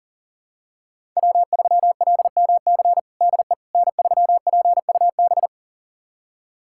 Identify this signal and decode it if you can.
Morse code, heard as a keyed tone, about 40 wpm, 700 Hz. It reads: W3LMC DE N3PUB
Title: USA Callsign DE Callsign - 5 Characters - 40wpm